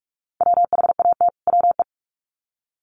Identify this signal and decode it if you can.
Morse code, heard as a keyed tone, about 44 wpm, 700 Hz. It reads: WHAT WE
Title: Sets of 2 Words 45wpm